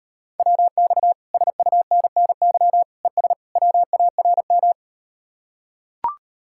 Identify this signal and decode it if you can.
Morse code, heard as a keyed tone, about 38 wpm, 700 Hz. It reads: WX SUNNY ES WARM